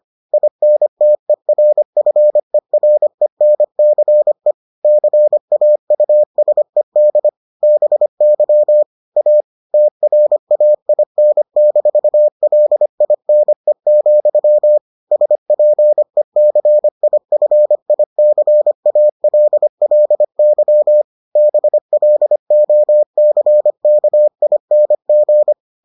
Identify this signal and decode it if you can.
Morse code, heard as a keyed tone, about 25 wpm, 600 Hz. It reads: INTERFERENCE CAUSED BY A TRAIN-LINE, SPECIFICALLY BLOCKING